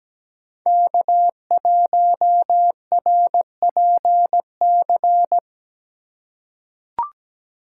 Morse code, heard as a keyed tone, about 17 wpm, 700 Hz. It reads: K1RPC
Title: USA Call Signs - 2x Character Spacing - 17wpm